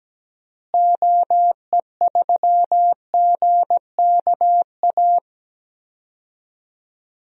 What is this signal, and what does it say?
Morse code, heard as a keyed tone, about 17 wpm, 700 Hz. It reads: OE3GKA
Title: DXCC-100 Call Signs - 6 Characters - Repeated 2x - 17wpm